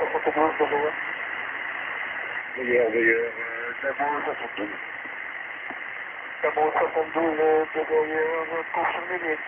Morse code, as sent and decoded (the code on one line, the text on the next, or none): none